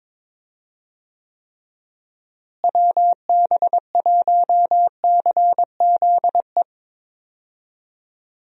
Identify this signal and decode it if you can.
Morse code, heard as a keyed tone, about 22 wpm, 700 Hz. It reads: WB1CZE
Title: USA Call Signs - 6 Characters - Repeated 3x - 22wpm